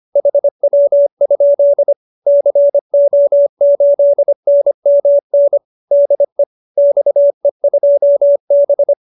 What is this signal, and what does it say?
Morse code, heard as a keyed tone, about 25 wpm, 570 Hz. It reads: HW? CO8NMN DE XE2B
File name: QSO304 25